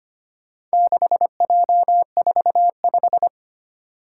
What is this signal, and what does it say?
Morse code, heard as a keyed tone, about 25 wpm, 700 Hz. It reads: 6J45
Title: Four Challenging Letter-Number Combinations 25wpm